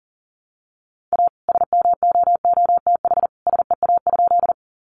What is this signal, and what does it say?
Morse code, heard as a keyed tone, about 40 wpm, 700 Hz. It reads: A SMOOTH SEA?
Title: Sentences from Top 1400 Words 40wpm